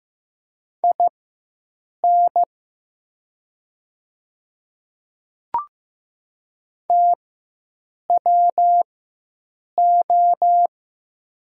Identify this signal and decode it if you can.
Morse code, heard as a keyed tone, about 15 wpm, spaced as for 7 wpm, 700 Hz. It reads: IN TWO
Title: 4x Character Spacing - Top 100 Words - 15wpm